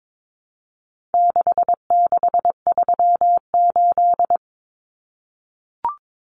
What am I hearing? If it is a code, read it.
Morse code, heard as a keyed tone, about 22 wpm, 700 Hz. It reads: 6638
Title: Numbers - 4-Digit 22wpm